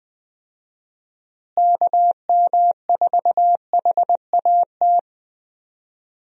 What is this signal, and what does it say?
Morse code, heard as a keyed tone, about 20 wpm, 700 Hz. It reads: KM4HAT
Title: USA Call Signs - 6 Characters - Repeated 2x - 20wpm